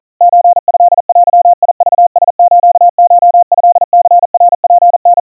Morse code (Japanese Wabun code), heard as a keyed tone, about 40 wpm, 700 Hz. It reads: ソトデクラスコトニナツタ
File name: kawa / Morse sample / kamyu 150 cpm 700 Hz Crispy